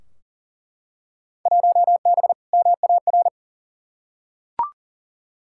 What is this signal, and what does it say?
Morse code, heard as a keyed tone, about 40 wpm, 700 Hz. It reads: 1B MAR
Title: Field Day Exchanges Encore - 40wpm